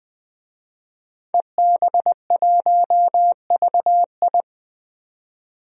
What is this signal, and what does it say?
Morse code, heard as a keyed tone, about 20 wpm, 700 Hz. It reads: EB1VI
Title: DXCC-100 Call Signs - 5 Characters - 20wpm